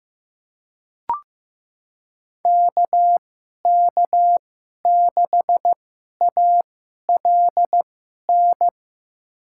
Morse code, heard as a keyed tone, about 15 wpm, spaced as for 11 wpm, 700 Hz. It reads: KK6ALN
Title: USA Call Signs - 6 Characters - 2x Character Spacing - 15wpm